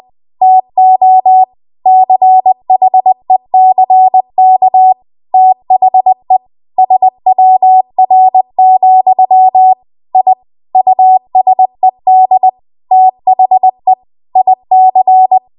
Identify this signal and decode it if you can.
Morse code, heard as a keyed tone, about 20 wpm, 750 Hz. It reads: TO CHECK THE SWR, I USED THE IC
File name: Archive / 20 WPM / 20WPM